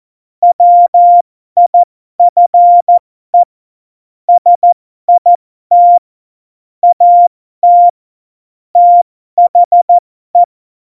Morse code, heard as a keyed tone, about 14 wpm, spaced as for 12 wpm, 700 Hz. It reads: WIFE SIT AT THE
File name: Morsekurs / Witze